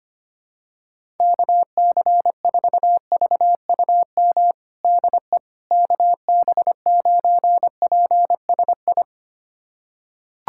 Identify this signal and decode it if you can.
Morse code, heard as a keyed tone, about 25 wpm, 700 Hz. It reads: KC4VUM DE KB9PSI
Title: USA Callsign DE Callsign - 6 Characters - 25wpm